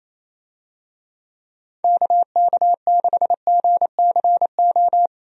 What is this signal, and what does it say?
Morse code, heard as a keyed tone, about 28 wpm, 700 Hz. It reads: KK6GCO
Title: USA Call Signs - 6 Characters - 28wpm